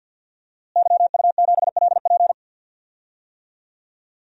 Morse code, heard as a keyed tone, about 47 wpm, 700 Hz. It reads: YU7LP
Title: DXCC-100 Call Signs - 5 Characters - Repeated 2x - 50wpm